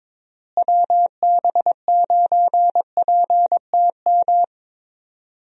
Morse code, heard as a keyed tone, about 22 wpm, 700 Hz. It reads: WB9PTM
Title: USA Call Signs - 6 Characters - Repeated 3x - 22wpm